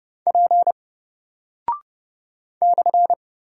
Morse code, heard as a keyed tone, about 30 wpm, 700 Hz. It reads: P /
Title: Single Character (Letter:Number:Punctuation:Prosign) Alt 30wpm